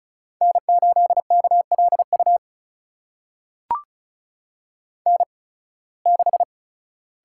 Morse code, heard as a keyed tone, about 36 wpm, spaced as for 23 wpm, 690 Hz. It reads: N8KLU N6
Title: USA Call Signs - 5 Characters - 8x Character Spacing - 35wpm